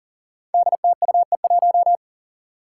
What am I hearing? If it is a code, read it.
Morse code, heard as a keyed tone, about 40 wpm, 700 Hz. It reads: DTUE1
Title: Five Challenging Letter-Number Combinations 40wpm